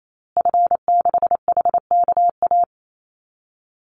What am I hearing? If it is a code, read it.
Morse code, heard as a keyed tone, about 28 wpm, 700 Hz. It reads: F6HKA
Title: Call Signs Repeated 2x - 28wpm